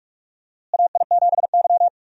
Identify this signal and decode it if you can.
Morse code, heard as a keyed tone, about 46 wpm, 700 Hz. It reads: AI7Y
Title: USA Call Signs - 4 Characters - Repeated 3x - 45wpm